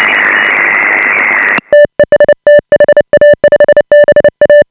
Morse code, heard as a keyed tone, about 29 wpm, 590 Hz. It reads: TEST HA5BA
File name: AUDIO2018 / HA5BA CW